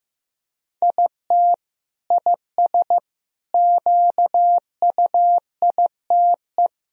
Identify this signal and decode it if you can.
Morse code, heard as a keyed tone, about 15 wpm, 700 Hz. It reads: IT IS QUITE
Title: Sentences from Top 1000 Words - Review Part 2 - 15wpm